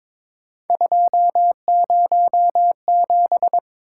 Morse code, heard as a keyed tone, about 22 wpm, 700 Hz. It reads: 207
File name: practice / 22wpm / Three Letter-Number Combinations - 22wpm